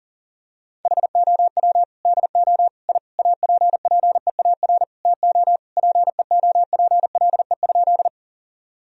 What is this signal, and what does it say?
Morse code, heard as a keyed tone, about 40 wpm, 700 Hz. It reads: HOW DO I APPEAR TO PEOPLE?
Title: Sentences from Top 1000 Words - Review Part 2 - 40wpm